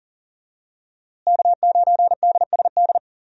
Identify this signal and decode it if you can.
Morse code, heard as a keyed tone, about 40 wpm, 700 Hz. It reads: K9DSD
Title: USA Call Signs - 5 Characters - Repeated 3x - 40wpm